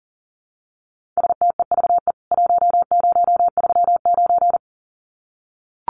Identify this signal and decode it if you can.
Morse code, heard as a keyed tone, about 40 wpm, 700 Hz. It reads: STEVE 1039
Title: CWT Exchange 40wpm